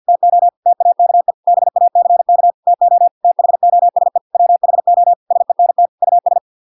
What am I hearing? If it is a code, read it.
Morse code, heard as a keyed tone, about 50 wpm, 700 Hz. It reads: TO TAKE BACK TO THOSE WHO SENT US